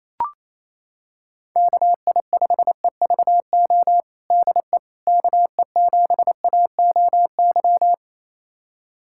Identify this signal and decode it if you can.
Morse code, heard as a keyed tone, about 28 wpm, 700 Hz. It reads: KI5EVO DE KE7AOY
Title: USA Callsign DE Callsign 28wpm